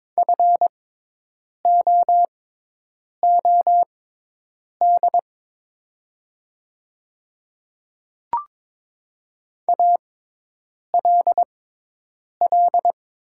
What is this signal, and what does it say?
Morse code, heard as a keyed tone, about 22 wpm, spaced as for 8 wpm, 700 Hz. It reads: FOOD ALL
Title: 6x Character Spacing - Top 500 Words - 22wpm